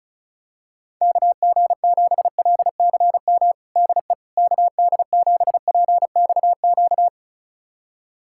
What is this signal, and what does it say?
Morse code, heard as a keyed tone, about 34 wpm, 700 Hz. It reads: KG7LCM DE KD7PXQ
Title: USA Callsign DE Callsign - 6 Characters - 35wpm